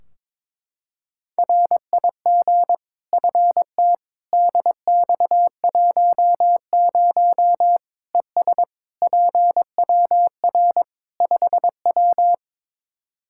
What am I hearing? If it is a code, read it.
Morse code, heard as a keyed tone, about 22 wpm, 700 Hz. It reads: RIG FT DX10 ES PWR 5W
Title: Practice QSO - Exchange 2 - Info - 22wpm